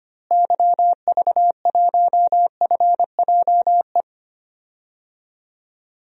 Morse code, heard as a keyed tone, about 25 wpm, 700 Hz. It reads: YV1FJE
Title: DXCC-100 Call Signs - 6 Characters - Repeated 3x - 25wpm